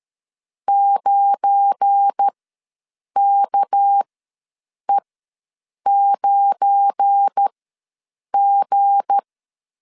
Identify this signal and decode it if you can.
Morse code, heard as a keyed tone, about 13 wpm, spaced as for 7 wpm, 800 Hz. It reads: 9KE9G
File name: LEZIONI / 7 GF9Z / 7 GF9 ws 12wpm 3344